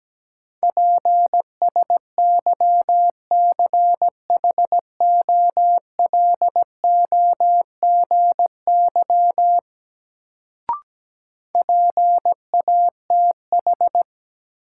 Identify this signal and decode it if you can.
Morse code, heard as a keyed tone, about 17 wpm, 700 Hz. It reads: PSYCHOLOGY PATH
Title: Sentences from Top 1400 Words - New Word Drill - 17wpm